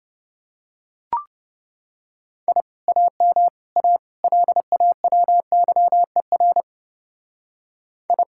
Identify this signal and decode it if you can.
Morse code, heard as a keyed tone, about 31 wpm, 700 Hz. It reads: I AM A LAWYER I